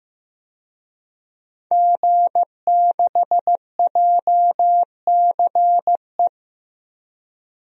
Morse code, heard as a keyed tone, about 16 wpm, 700 Hz. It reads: G6JCE
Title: DXCC-100 Call Signs - 5 Characters - Repeated 2x - 15wpm